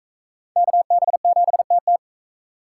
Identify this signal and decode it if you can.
Morse code, heard as a keyed tone, about 44 wpm, 700 Hz. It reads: KB7TT